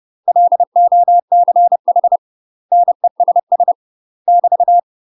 Morse code, heard as a keyed tone, about 30 wpm, 700 Hz. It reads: LOCH NESS <BT>
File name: UK MASTER QSOs TRAINING / 30WPM / UK MASTER QSOs 0035